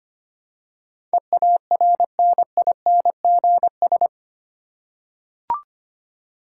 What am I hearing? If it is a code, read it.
Morse code, heard as a keyed tone, about 25 wpm, 700 Hz. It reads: EARNINGS
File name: 8 Letter Words - First 3 Letters Given - 25wpm